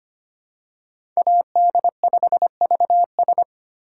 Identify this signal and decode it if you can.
Morse code, heard as a keyed tone, about 25 wpm, 700 Hz. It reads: AD5VS
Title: USA Call Signs - 6x Character Spacing - 25wpm